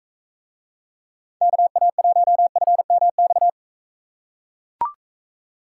Morse code, heard as a keyed tone, about 41 wpm, 700 Hz. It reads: KA1FMX